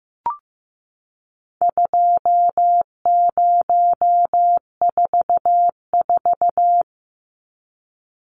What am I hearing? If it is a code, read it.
Morse code, heard as a keyed tone, about 15 wpm, 700 Hz. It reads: 2044
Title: Numbers - 4-Digit 15wpm